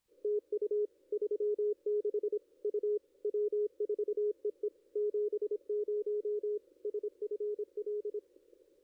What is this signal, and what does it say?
Morse code, heard as a keyed tone, about 25 wpm, 410 Hz, with radio noise. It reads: TU 36 U W4EE 70 SFL
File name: contests / ss-cw-k5tr-op-n5rz -2300